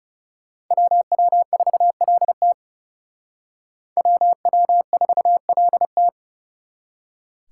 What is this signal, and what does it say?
Morse code, heard as a keyed tone, about 37 wpm, 700 Hz. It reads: WW4LT WW4LT
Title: Speed Racing - USA 2x2 Call Signs 30wpm